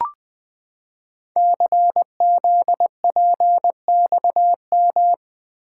Morse code, heard as a keyed tone, about 20 wpm, 700 Hz. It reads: CZPXM
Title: Code Groups - 5 Letters - 20wpm